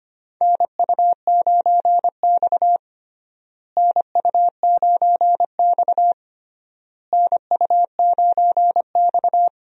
Morse code, heard as a keyed tone, about 25 wpm, 700 Hz. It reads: NU9X NU9X NU9X
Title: USA Call Signs - 4 Characters - Repeated 3x - 25wpm